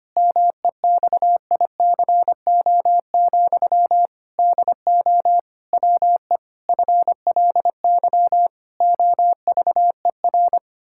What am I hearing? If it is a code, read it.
Morse code, heard as a keyed tone, about 25 wpm, 700 Hz. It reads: MEXICO, DO WE FLY OVER
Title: Pangram Challenge v2 - 25wpm